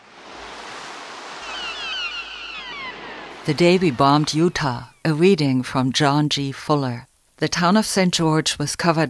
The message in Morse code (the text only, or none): none